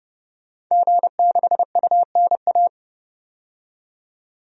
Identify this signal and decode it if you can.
Morse code, heard as a keyed tone, about 31 wpm, 690 Hz. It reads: G6UNA
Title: DXCC-100 Call Signs - 5 Characters - 30wpm